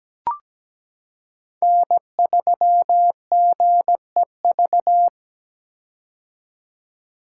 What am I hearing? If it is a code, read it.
Morse code, heard as a keyed tone, about 17 wpm, 700 Hz. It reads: N3GEV
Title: Code Groups - 5 Letter:Numbers - 17wpm